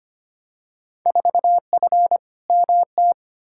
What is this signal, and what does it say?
Morse code, heard as a keyed tone, about 25 wpm, 700 Hz. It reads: 4F MT
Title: Field Day Exchanges Encore - 25wpm